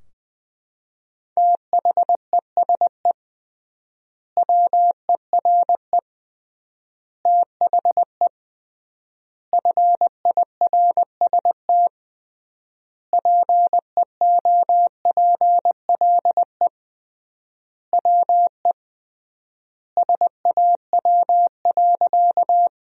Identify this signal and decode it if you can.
Morse code, heard as a keyed tone, about 20 wpm, 700 Hz. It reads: THESE WERE THE FIRST PEOPLE WE SAW.
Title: 3x Word-Spacing - Sentences from Top 300 Words - 20wpm